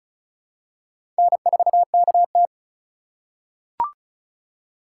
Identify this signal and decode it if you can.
Morse code, heard as a keyed tone, about 36 wpm, 700 Hz. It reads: N4KT
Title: USA Call Signs - 4 Characters - 3x Character Spacing - 35wpm